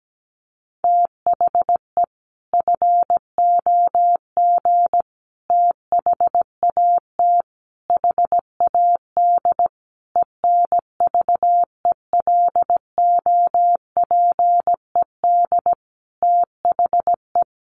Morse code, heard as a keyed tone, about 17 wpm, 700 Hz. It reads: THE FOG THAT HAD ENVELOPED THE
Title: A Sound Of Thunder 17wpm